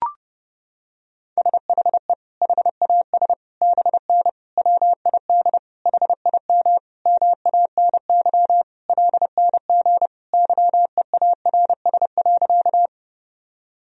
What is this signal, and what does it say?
Morse code, heard as a keyed tone, about 32 wpm, 700 Hz. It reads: SHE HAS BN WID HIM MANY LNG YEARS.
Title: Sentences from Top 200 Words and Abbreviations 30wpm